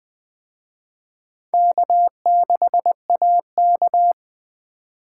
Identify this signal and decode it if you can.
Morse code, heard as a keyed tone, about 21 wpm, 700 Hz. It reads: K6AK